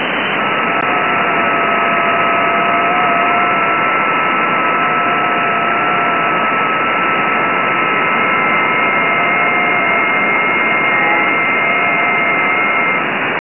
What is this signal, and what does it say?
FT8.